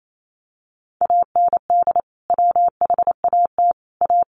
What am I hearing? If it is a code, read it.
Morse code, heard as a keyed tone, about 28 wpm, 700 Hz. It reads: AND WHAT A
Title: Sentences from Top 600 Words 28wpm